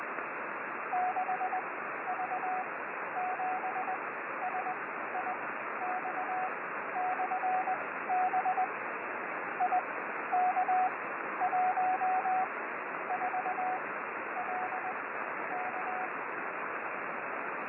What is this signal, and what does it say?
Morse code, heard as a keyed tone, about 20 wpm, spaced as for 12 wpm, 740 Hz, with radio noise. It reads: 6V7SIX/B IK14LK
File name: 6V7SIX B